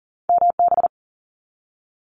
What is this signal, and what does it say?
Morse code, heard as a keyed tone, about 41 wpm, 700 Hz. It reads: MB
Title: Canadian Provinces and Territories 40wpm